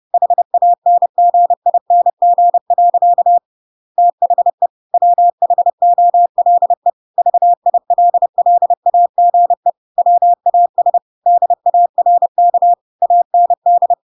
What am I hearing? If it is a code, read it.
Morse code, heard as a keyed tone, about 30 wpm, 700 Hz. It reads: HANGING. THE WHOLE VILLAGE WAS DARK AND
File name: Pinocchio / Pinocchio ch6 30 30 ik1hge